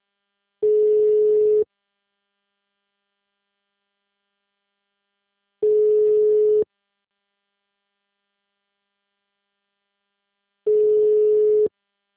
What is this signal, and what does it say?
Ringback tone (Europe, 425 Hz).
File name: out- -2002- - - .5457